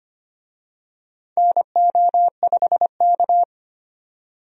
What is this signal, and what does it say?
Morse code, heard as a keyed tone, about 25 wpm, 700 Hz. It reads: NO5K
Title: USA Call Signs - 4 Characters - 25wpm